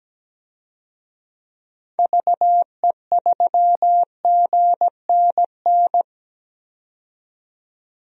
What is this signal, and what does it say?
Morse code, heard as a keyed tone, about 17 wpm, 700 Hz. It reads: VE3GNN
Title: DXCC-100 Call Signs - 6 Characters - 17wpm